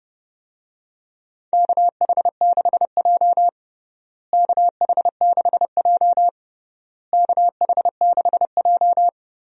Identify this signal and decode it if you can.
Morse code, heard as a keyed tone, about 30 wpm, 700 Hz. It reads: KH6J KH6J KH6J
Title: USA Call Signs - 4 Characters - Repeated 3x - 30wpm